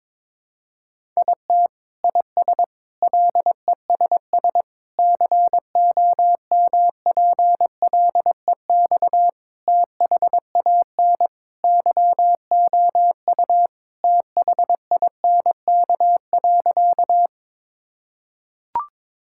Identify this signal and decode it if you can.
Morse code, heard as a keyed tone, about 22 wpm, 700 Hz. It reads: IT IS LESS COMPLEX THAN YOU THINK.
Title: Sentences from Top 1000 Words - Review Part 6 - 22wpm